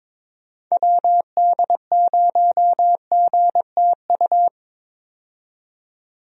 Morse code, heard as a keyed tone, about 22 wpm, 700 Hz. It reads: WD0GTU
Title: USA Call Signs - 6 Characters - 22wpm